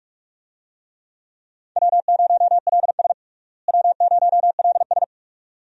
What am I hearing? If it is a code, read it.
Morse code, heard as a keyed tone, about 45 wpm, 700 Hz. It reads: W0LS W0LS
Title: USA Call Signs - 4 Characters - Repeated 2x - 45wpm